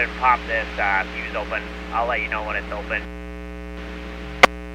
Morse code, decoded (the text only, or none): none